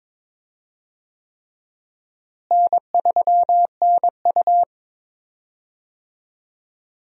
Morse code, heard as a keyed tone, about 22 wpm, 700 Hz. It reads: N3NU
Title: USA Call Signs - 4 Characters - Repeated 3x - 22wpm